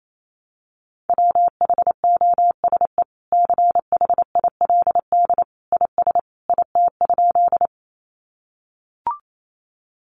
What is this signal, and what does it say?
Morse code, heard as a keyed tone, about 28 wpm, 700 Hz. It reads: WHOSE CHILD IS IT?
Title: Sentences from Top 1200 Words 28wpm